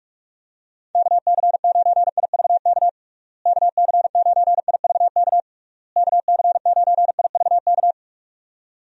Morse code, heard as a keyed tone, about 46 wpm, 700 Hz. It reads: KC9IVK KC9IVK KC9IVK
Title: USA Call Signs - 6 Characters - Repeated 3x - 45wpm